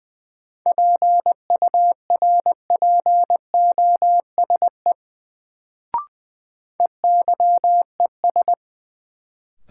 Morse code, heard as a keyed tone, about 21 wpm, 700 Hz. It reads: PURPOSE EYES
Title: Sets of 5 Words - New Word Drill - 20wpm